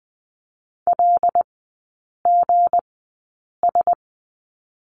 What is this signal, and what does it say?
Morse code, heard as a keyed tone, about 20 wpm, spaced as for 8 wpm, 700 Hz. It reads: LGS
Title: 159 Lesson 26e - Letter g - Three Characters 20wpm